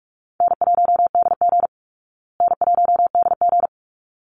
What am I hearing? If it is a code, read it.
Morse code, heard as a keyed tone, about 44 wpm, 700 Hz. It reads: N1DG N1DG
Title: Easy Call Signs Repeated 2x - 45wpm